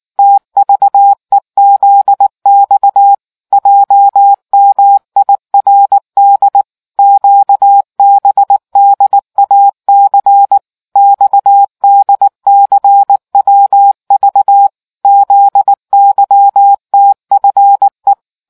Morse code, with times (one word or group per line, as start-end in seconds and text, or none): 0.2-3.1 TVEZX
3.5-6.6 JMIRD
7.0-10.6 QBDAC
11.0-14.7 XDCWV
15.1-18.1 ZYTFE